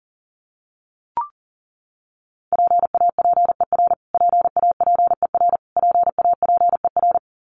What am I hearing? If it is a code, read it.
Morse code, heard as a keyed tone, about 40 wpm, 700 Hz. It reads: PAPER PAPER PAPER
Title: Top 500 Words Repeated 40wpm